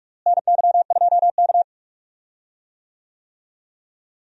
Morse code, heard as a keyed tone, about 42 wpm, 700 Hz. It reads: NY2K
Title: USA Call Signs - 4 Characters - Repeated 3x - 45wpm